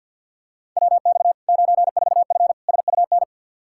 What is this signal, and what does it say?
Morse code, heard as a keyed tone, about 52 wpm, 700 Hz. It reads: WX 94F SUN
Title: Practice QSO - Exchange 2 - Info - 50wpm